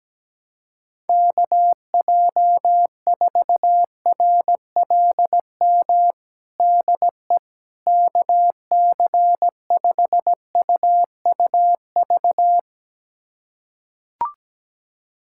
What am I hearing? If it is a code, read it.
Morse code, heard as a keyed tone, about 17 wpm, 700 Hz. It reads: KJ4RLM DE KC5UUV
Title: USA Callsign DE Callsign - 6 Characters - 17wpm